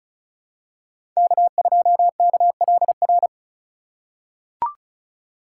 Morse code, heard as a keyed tone, about 36 wpm, 700 Hz. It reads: K2KLR
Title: USA Call Signs - 5 Characters - 2x Character Spacing - 35wpm